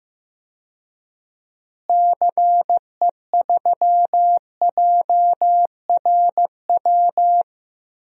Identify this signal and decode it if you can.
Morse code, heard as a keyed tone, about 15 wpm, 700 Hz. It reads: CE3JRW
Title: DXCC-100 Call Signs - 6 Characters - Repeated 2x - 15wpm